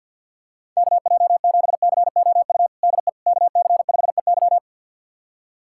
Morse code, heard as a keyed tone, about 52 wpm, 690 Hz. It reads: KJ7COU DE KC5EY
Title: USA Callsign DE Callsign 50wpm